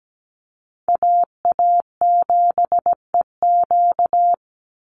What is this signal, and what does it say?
Morse code, heard as a keyed tone, about 17 wpm, 700 Hz. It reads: AA7EQ
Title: NA SOTA Chaser Callsigns 17wpm